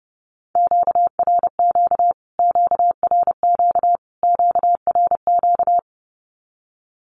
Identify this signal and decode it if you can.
Morse code, heard as a keyed tone, about 30 wpm, 700 Hz. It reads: QRQ QRQ QRQ
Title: Q-Codes Repeated 3x 30wpm